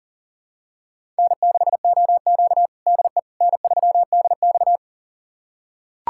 Morse code, heard as a keyed tone, about 40 wpm, 700 Hz. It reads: N6OQ DE N3DX
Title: USA Callsign DE Callsign - 4 Characters - 40wpm